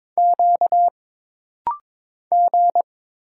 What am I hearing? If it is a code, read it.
Morse code, heard as a keyed tone, about 23 wpm, 700 Hz. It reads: Q G
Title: Single Letters - Rapid-Fire 22wpm